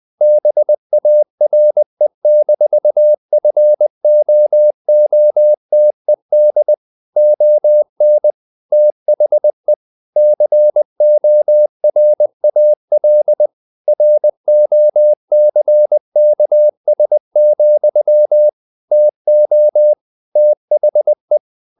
Morse code, heard as a keyed tone, about 20 wpm, 600 Hz. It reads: BARE-FOOTED ON THE CORAL ROCKS, TO THE
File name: the old man and the sea / the old man and the sea part 1 20wpm